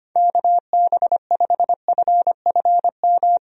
Morse code, heard as a keyed tone, about 25 wpm, 700 Hz. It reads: KB5FFM
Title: USA Call Signs - 6 Characters - 25wpm